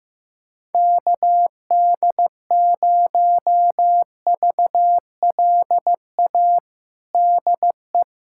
Morse code, heard as a keyed tone, about 15 wpm, 700 Hz. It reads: KD0VLA DE
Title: USA Callsign DE Callsign - 6 Characters - 15wpm